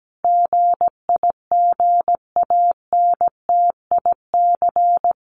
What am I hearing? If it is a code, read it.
Morse code, heard as a keyed tone, about 17 wpm, 700 Hz. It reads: GIGANTIC
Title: A Sound Of Thunder 17wpm